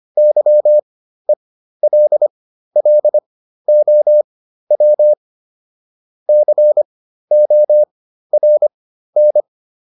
Morse code, heard as a keyed tone, about 25 wpm, spaced as for 13 wpm, 600 Hz. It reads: YELLOW CORN